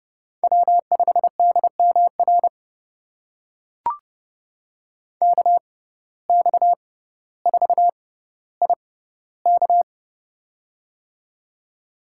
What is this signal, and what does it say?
Morse code, heard as a keyed tone, about 31 wpm, spaced as for 15 wpm, 700 Hz. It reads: W5DMR KX4IK
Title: USA Call Signs - 5 Characters - 6x Character Spacing - 30wpm